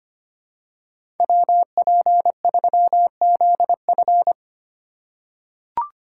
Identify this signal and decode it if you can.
Morse code, heard as a keyed tone, about 25 wpm, 700 Hz. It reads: WP3ZF
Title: USA Call Signs - 5 Characters - 2x Character Spacing - 25wpm